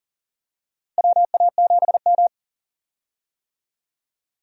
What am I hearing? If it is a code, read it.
Morse code, heard as a keyed tone, about 40 wpm, 700 Hz. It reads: WA7M
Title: USA Call Signs - 4 Characters - Repeated 2x - 40wpm